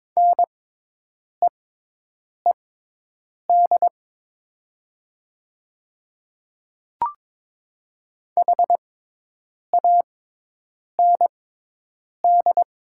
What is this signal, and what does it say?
Morse code, heard as a keyed tone, about 22 wpm, spaced as for 8 wpm, 700 Hz. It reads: NEED HAND
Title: 6x Character Spacing - Top 300 Words - 22wpm